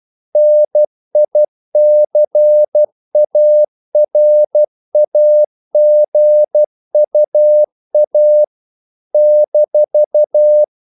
Morse code, heard as a keyed tone, about 12 wpm, 600 Hz. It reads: NICARAGUA -
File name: CW / Practice / NationalCapitals12